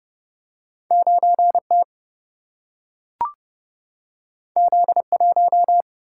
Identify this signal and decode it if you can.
Morse code, heard as a keyed tone, about 30 wpm, 700 Hz. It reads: 9T Z1
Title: Single-Letter Single-Number 30wpm